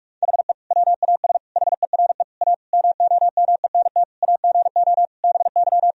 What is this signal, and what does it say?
Morse code, heard as a keyed tone, about 46 wpm, 700 Hz. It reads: HE WAS HERE A MOMENT AGO BY
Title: Sentences from Top 1000 Words - Review Part 3 - 45wpm